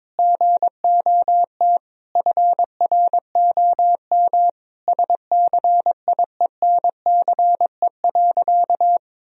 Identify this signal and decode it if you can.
Morse code, heard as a keyed tone, about 22 wpm, 700 Hz. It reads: GOT FROM SCIENCE.
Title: Sentences from Top 1000 Words - Review Part 2 - 22wpm